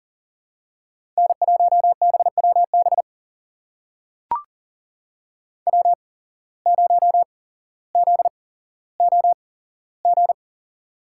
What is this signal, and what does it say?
Morse code, heard as a keyed tone, about 40 wpm, spaced as for 17 wpm, 700 Hz. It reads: N1BWB W0ZOG
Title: USA Call Signs - 5 Characters - 8x Character Spacing - 40wpm